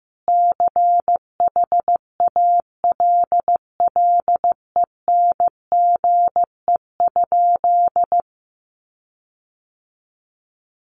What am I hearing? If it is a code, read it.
Morse code, heard as a keyed tone, about 15 wpm, 700 Hz. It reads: CHALLENGE?
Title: Sentences from Top 1300 Words 15wpm